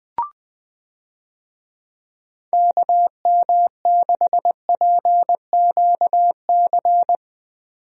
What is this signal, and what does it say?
Morse code, heard as a keyed tone, about 20 wpm, 700 Hz. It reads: KM6PQC
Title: USA Call Signs - 6 Characters - 20wpm